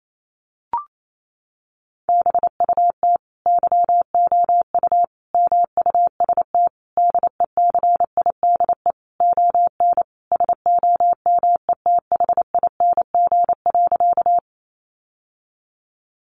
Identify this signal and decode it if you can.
Morse code, heard as a keyed tone, about 28 wpm, 700 Hz. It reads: BUT YOU MUST DECIDE ON SOMETHING.
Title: Sentences from Top 600 Words 28wpm